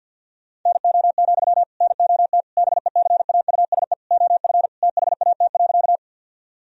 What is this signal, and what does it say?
Morse code, heard as a keyed tone, about 50 wpm, 700 Hz. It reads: NO, NOT BECAUSE OF THAT.
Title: Sentences from Top 1000 Words - Review Part 3 - 50wpm